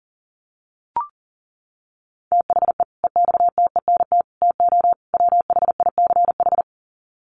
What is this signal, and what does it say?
Morse code, heard as a keyed tone, about 40 wpm, 700 Hz. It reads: THE EXTENT TO WHICH
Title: Sets of 4 Words 40wpm